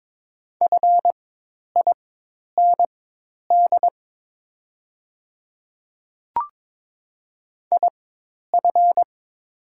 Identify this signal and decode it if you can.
Morse code, heard as a keyed tone, about 22 wpm, spaced as for 10 wpm, 700 Hz. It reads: FIND IF
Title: 4x Character Spacing - Top 100 Words - 22wpm